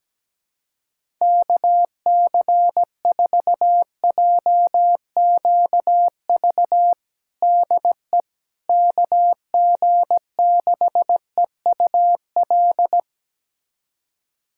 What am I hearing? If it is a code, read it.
Morse code, heard as a keyed tone, about 17 wpm, 700 Hz. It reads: KC4JQV DE KG6EUL
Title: USA Callsign DE Callsign - 6 Characters - 17wpm